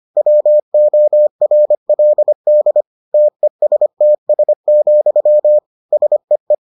Morse code, heard as a keyed tone, about 25 wpm, 600 Hz. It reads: WORLD TESTS, SEE